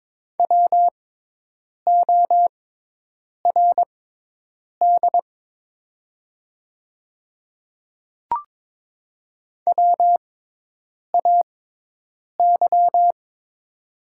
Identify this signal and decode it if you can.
Morse code, heard as a keyed tone, about 22 wpm, spaced as for 8 wpm, 700 Hz. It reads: WORD WAY
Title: 6x Character Spacing - Top 200 Words - 22wpm